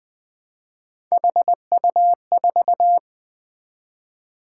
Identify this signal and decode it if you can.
Morse code, heard as a keyed tone, about 20 wpm, 700 Hz. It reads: HU4
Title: Three Letter-Number Combinations - 20wpm